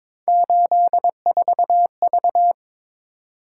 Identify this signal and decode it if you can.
Morse code, heard as a keyed tone, about 27 wpm, 700 Hz. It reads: 84V
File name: Speed-Racing - Three Letter-Number Combinations - 22wpm